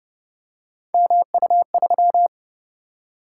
Morse code, heard as a keyed tone, about 32 wpm, 700 Hz. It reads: MU3